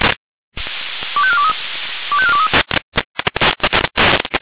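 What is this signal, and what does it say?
Five-tone selective call (ZVEI), heard as quick runs of five tones, tones of about 70 ms. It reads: 25423 25423
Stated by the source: ZVEI five-tone selective call received on SDR.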